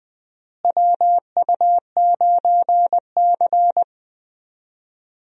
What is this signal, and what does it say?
Morse code, heard as a keyed tone, about 21 wpm, 700 Hz. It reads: WU9C
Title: USA Call Signs - 4 Characters - Repeated 2x - 20wpm